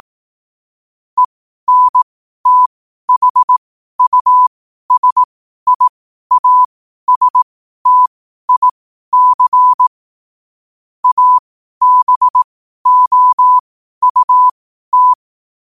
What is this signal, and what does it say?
Morse code, heard as a keyed tone, about 18 wpm, spaced as for 12 wpm, 1000 Hz. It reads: ENTHUSIASTIC ABOUT
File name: qotdfiles / 12wpm